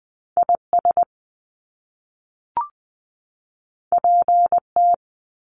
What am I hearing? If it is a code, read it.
Morse code, heard as a keyed tone, about 19 wpm, 700 Hz. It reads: IS PT
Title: Top Two Letter Combinations in Words 20wpm